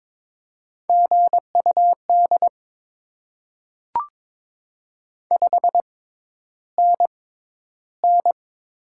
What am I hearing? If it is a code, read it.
Morse code, heard as a keyed tone, about 22 wpm, spaced as for 11 wpm, 700 Hz. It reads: GUD 5NN
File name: Top 20 QSO Elements - 6x Character Spacing - 22wpm